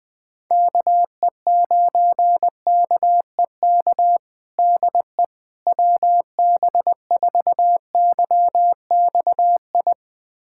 Morse code, heard as a keyed tone, about 21 wpm, 700 Hz. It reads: KE9KEK DE WB4YXI
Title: USA Callsign DE Callsign - 6 Characters - 20wpm